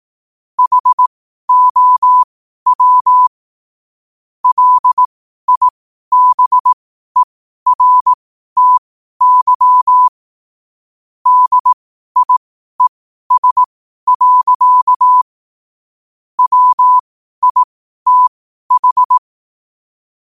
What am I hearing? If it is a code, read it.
Morse code, heard as a keyed tone, about 18 wpm, spaced as for 12 wpm, 1000 Hz. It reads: HOW LIBERTY DIES. WITH
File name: qotdfiles / 12wpm